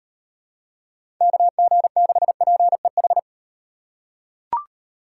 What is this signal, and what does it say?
Morse code, heard as a keyed tone, about 38 wpm, 700 Hz. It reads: KG6PEH